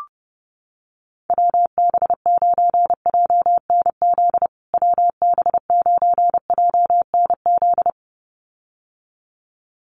Morse code, heard as a keyed tone, about 28 wpm, 700 Hz. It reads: WB9JNZ WB9JNZ
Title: USA Call Signs Repeated 2x 30wpm